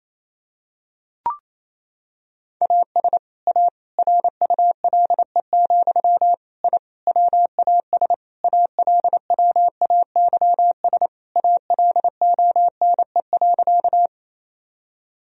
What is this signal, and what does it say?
Morse code, heard as a keyed tone, about 28 wpm, 700 Hz. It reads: AS A RULE, I WAS ALWAYS ALONE.
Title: Sentences from Top 1000 Words - Review Part 2 - 28wpm